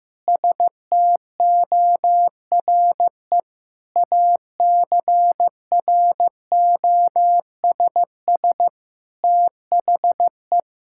Morse code, heard as a keyed tone, about 15 wpm, 700 Hz. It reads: STORE ACROSS THE
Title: Sentences from Top 1000 Words - Review Part 5 - 15wpm